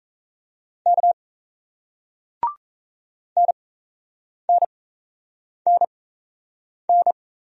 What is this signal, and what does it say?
Morse code, heard as a keyed tone, about 38 wpm, spaced as for 8 wpm, 700 Hz. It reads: K NNNN